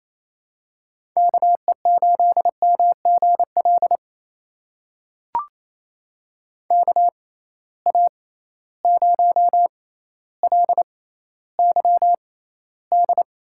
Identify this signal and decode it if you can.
Morse code, heard as a keyed tone, about 28 wpm, spaced as for 14 wpm, 700 Hz. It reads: KE8MGL KA0LYD
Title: USA Call Signs - 6 Characters - 6x Character Spacing - 28wpm